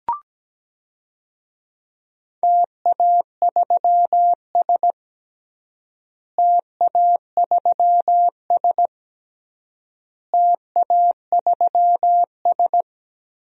Morse code, heard as a keyed tone, about 17 wpm, 700 Hz. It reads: TA3S TA3S TA3S
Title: DXCC-100 Call Signs - 4 Characters - Repeated 3x - 17wpm